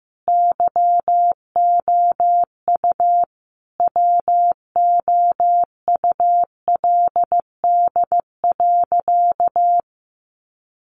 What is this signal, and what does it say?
Morse code, heard as a keyed tone, about 14 wpm, 700 Hz. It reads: YOU WOULD.
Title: Sentences from Top 1500 Words 15wpm